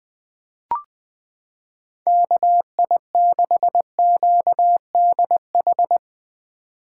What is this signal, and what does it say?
Morse code, heard as a keyed tone, about 20 wpm, 700 Hz. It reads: KI6QDH
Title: K1USN SST QSO Practice - 20wpm